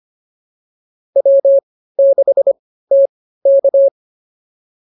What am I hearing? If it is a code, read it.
Morse code, heard as a keyed tone, about 25 wpm, spaced as for 15 wpm, 590 Hz. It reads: W6TK